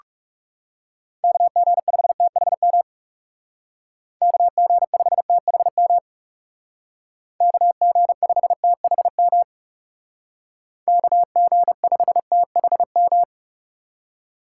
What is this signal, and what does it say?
Morse code, heard as a keyed tone, about 37 wpm, 700 Hz. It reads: KG5THM KG5THM KG5THM KG5THM
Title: Speed Racing - USA 2x3 Call Signs 30wpm